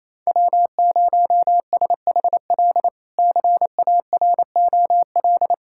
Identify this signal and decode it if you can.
Morse code, heard as a keyed tone, about 28 wpm, 700 Hz. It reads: W0SHL CAROL
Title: K1USN SST QSO Practice - 28wpm